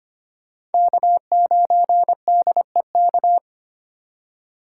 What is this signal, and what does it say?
Morse code, heard as a keyed tone, about 32 wpm, 700 Hz. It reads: K9DEK